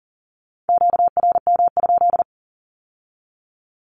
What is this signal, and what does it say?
Morse code, heard as a keyed tone, about 38 wpm, 700 Hz. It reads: QRM?
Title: Q-Codes with Spoken Meaning - 40wpm